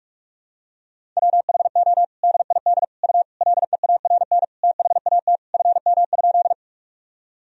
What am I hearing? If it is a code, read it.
Morse code, heard as a keyed tone, about 41 wpm, 700 Hz. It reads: WHO DID U LEARN THAT FM?
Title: Sentences from Top 300 Words and Abbreviations 45wpm